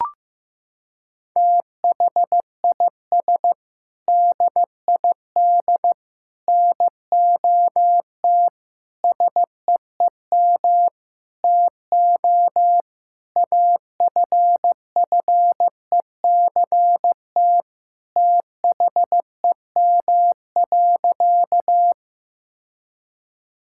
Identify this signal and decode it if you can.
Morse code, heard as a keyed tone, about 15 wpm, 700 Hz. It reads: THIS DID NOT SEEM TO AFFECT THEM.
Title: Sentences from Top 1000 Words - Review Part 6 - 15wpm